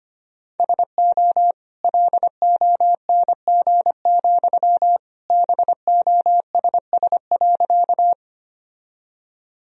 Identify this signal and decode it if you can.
Morse code, heard as a keyed tone, about 25 wpm, 700 Hz. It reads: SO LONG, BOSS.
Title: Sentences from Top 1000 Words - Review Part 4 - 25wpm